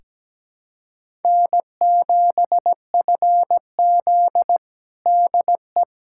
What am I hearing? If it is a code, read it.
Morse code, heard as a keyed tone, about 17 wpm, 700 Hz. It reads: N7FZ DE
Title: USA Callsign DE Callsign - 4 Characters - 17wpm